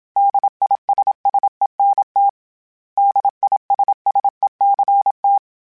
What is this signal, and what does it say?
Morse code, heard as a keyed tone, about 26 wpm, 800 Hz. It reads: DISSENT DISSECT
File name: General / General-mp3s / ISS @25